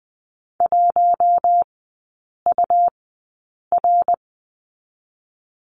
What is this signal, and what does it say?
Morse code, heard as a keyed tone, about 20 wpm, spaced as for 8 wpm, 700 Hz. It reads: 1UR
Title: 100 Lesson 17e - Letter u - Three Characters 20wpm